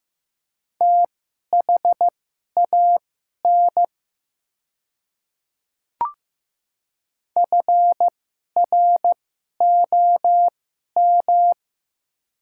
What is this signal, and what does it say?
Morse code, heard as a keyed tone, about 15 wpm, spaced as for 11 wpm, 700 Hz. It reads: THAN FROM
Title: 2x Character Spacing - Top 300 Words - 15wpm